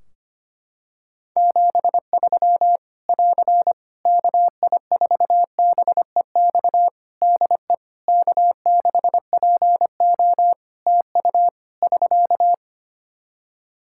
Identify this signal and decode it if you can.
Morse code, heard as a keyed tone, about 25 wpm, 700 Hz. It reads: 73 <AR> KI4BEX DE K6PO TU <SK>
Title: Practice QSO - Exchange 4 - Ending - 25wpm